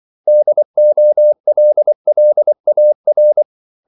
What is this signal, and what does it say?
Morse code, heard as a keyed tone, about 24 wpm, 600 Hz. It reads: DOLLAR
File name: CW / Practice / 500MostCommonEnglishWords24